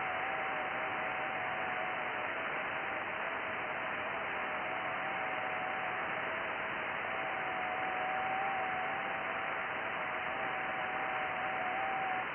FT8.
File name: IC-7100